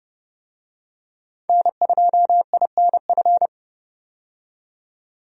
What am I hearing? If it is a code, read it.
Morse code, heard as a keyed tone, about 32 wpm, 700 Hz. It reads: N2INF
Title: USA Call Signs - 5 Characters - 30wpm